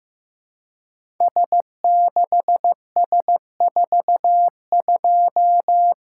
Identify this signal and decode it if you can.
Morse code, heard as a keyed tone, about 15 wpm, 700 Hz. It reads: S6S42
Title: Code Groups - 5 Letter:Numbers - 15wpm